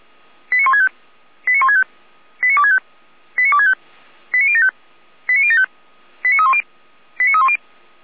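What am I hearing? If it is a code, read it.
Five-tone selective call (ZVEI), heard as quick runs of five tones, tones of about 80 ms. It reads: 89165 89165 89265 89265 89075 89075 89310 89310
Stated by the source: ZVEI five-tone selective call received by SDR.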